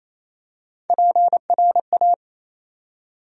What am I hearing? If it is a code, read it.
Morse code, heard as a keyed tone, about 27 wpm, 700 Hz. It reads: PRA
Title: USA Call Sign Postfixes - 3 Letters 28wpm